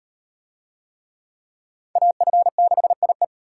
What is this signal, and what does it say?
Morse code, heard as a keyed tone, about 38 wpm, 700 Hz. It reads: AF6IE